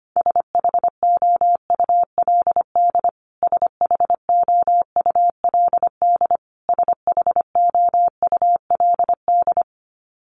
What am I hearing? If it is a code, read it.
Morse code, heard as a keyed tone, about 25 wpm, 700 Hz. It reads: SHOULD SHOULD SHOULD
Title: Top 500 Words Repeated 25wpm